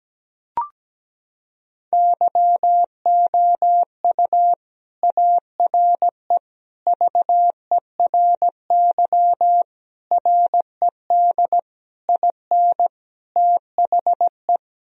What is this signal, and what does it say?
Morse code, heard as a keyed tone, about 17 wpm, 700 Hz. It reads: YOU ARE VERY RED IN THE
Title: Sentences from Top 1000 Words - Review Part 2 - 17wpm